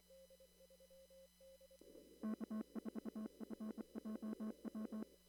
Morse code, heard as a keyed tone, about 24 wpm, 220 Hz. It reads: K4FJW
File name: CW / kt5j wpx cw-op-k5pi